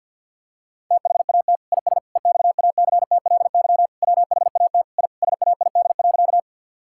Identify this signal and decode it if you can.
Morse code, heard as a keyed tone, about 50 wpm, 700 Hz. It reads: THAT IS EXACTLY WHAT I SAID.
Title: Sentences from Top 1000 Words - Review Part 6 - 50wpm